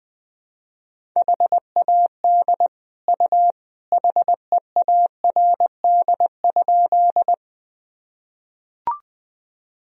Morse code, heard as a keyed tone, about 21 wpm, 700 Hz. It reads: HAD U HEARD?
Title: Sentences from Top 400 Words and Abbreviations 20wpm